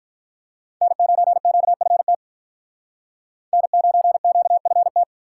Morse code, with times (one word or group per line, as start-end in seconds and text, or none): 0.8-2.1 N9QFT
3.5-5.0 N9QFT